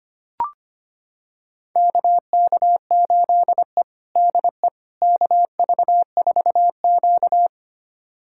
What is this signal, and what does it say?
Morse code, heard as a keyed tone, about 25 wpm, 700 Hz. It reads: KK8E DE KV4Q
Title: USA Callsign DE Callsign - 4 Characters - 25wpm